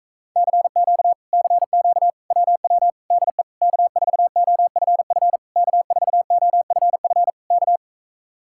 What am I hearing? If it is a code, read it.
Morse code, heard as a keyed tone, about 44 wpm, 700 Hz. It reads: CQ CQ WW DE K4OFF K4OFF K